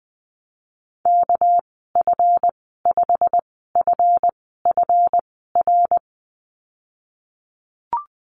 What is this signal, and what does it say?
Morse code, heard as a keyed tone, about 20 wpm, spaced as for 15 wpm, 700 Hz. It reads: KF5FFR
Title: 200 Lesson 32g - Letter k - Call Signs 20wpm